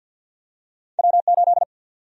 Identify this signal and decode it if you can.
Morse code, heard as a keyed tone, about 53 wpm, 700 Hz. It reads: W8
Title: USA Call Sign Prefixes - 50wpm